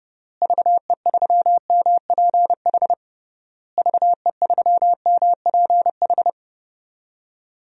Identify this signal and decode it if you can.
Morse code, heard as a keyed tone, about 30 wpm, 700 Hz. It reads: VE3MPH VE3MPH
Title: DXCC-100 Call Signs - 6 Characters - Repeated 3x - 30wpm